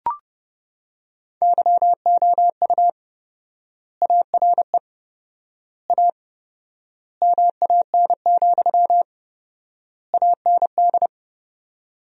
Morse code, heard as a keyed tone, about 30 wpm, 700 Hz. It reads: YOU ARE A MAN, AND
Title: 4x Word-Spacing - Sentences from Top 200 Words - 30wpm